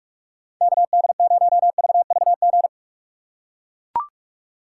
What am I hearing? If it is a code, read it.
Morse code, heard as a keyed tone, about 44 wpm, 700 Hz. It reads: KD0VVG
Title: USA Call Signs - 2x Character Spacing - 45wpm